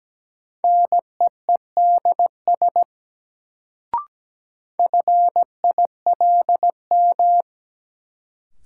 Morse code, heard as a keyed tone, about 17 wpm, 700 Hz. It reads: NEEDS FILM
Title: Sentences from Top 1000 Words - Review Part 3 - New Word Drill - 17wpm